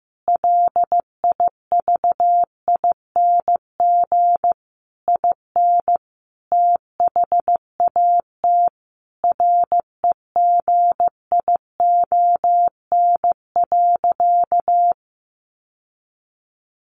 Morse code, heard as a keyed tone, about 15 wpm, 700 Hz. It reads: LIVING IN THAT REGION.
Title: Sentences from Top 1500 Words 15wpm